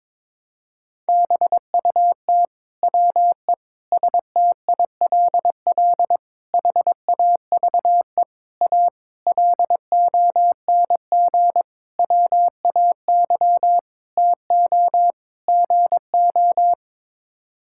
Morse code, heard as a keyed tone, about 22 wpm, 700 Hz. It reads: BUT WE STILL HAVE A LONG WAY TO GO
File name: Sets of 9 Words - 22wpm